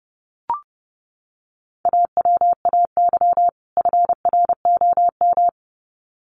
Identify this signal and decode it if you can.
Morse code, heard as a keyed tone, about 30 wpm, 700 Hz. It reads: AWAY FROM
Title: Sets of 2 Words 30wpm